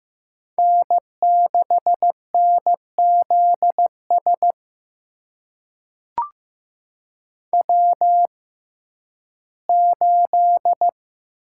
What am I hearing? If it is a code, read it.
Morse code, heard as a keyed tone, about 15 wpm, spaced as for 11 wpm, 700 Hz. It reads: N6NZS W8
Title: USA Call Signs - 6x Character Spacing - 15wpm